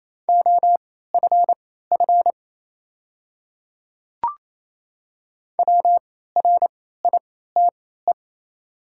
Morse code, heard as a keyed tone, about 28 wpm, spaced as for 16 wpm, 700 Hz. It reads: OFF WRITE
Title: 3x Character Spacing - Top 200 Words - 28wpm